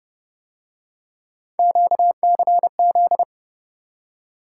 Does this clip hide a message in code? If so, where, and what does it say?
Yes: Morse code from 1.6 to 3.2 s, heard as a keyed tone, about 29 wpm, 700 Hz. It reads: QCZ